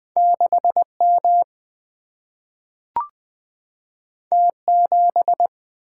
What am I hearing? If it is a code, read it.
Morse code, heard as a keyed tone, about 20 wpm, 700 Hz. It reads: 6 M T 7
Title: Single-Letter Single-Number 20wpm